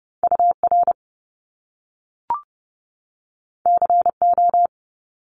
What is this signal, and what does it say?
Morse code, heard as a keyed tone, about 30 wpm, 700 Hz. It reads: UR CO
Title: Top Two Letter Combinations in Words 30wpm